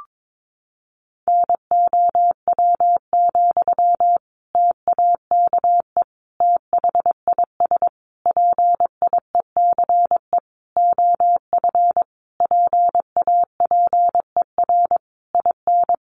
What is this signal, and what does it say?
Morse code, heard as a keyed tone, about 22 wpm, 700 Hz. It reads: NOW, TAKE THIS PIECE OF PAPER IN
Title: Sentences from Top 400 Words 22wpm